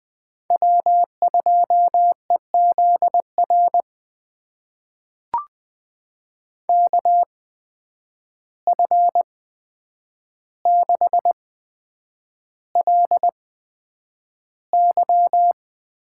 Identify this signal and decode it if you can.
Morse code, heard as a keyed tone, about 20 wpm, spaced as for 9 wpm, 700 Hz. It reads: W2EZR KF6LY
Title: USA Call Signs - 5 Characters - 8x Character Spacing - 20wpm